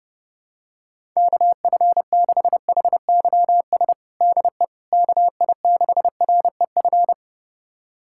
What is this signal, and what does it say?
Morse code, heard as a keyed tone, about 32 wpm, 700 Hz. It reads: KF6HYS DE KI6REF
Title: USA Callsign DE Callsign - 6 Characters - 30wpm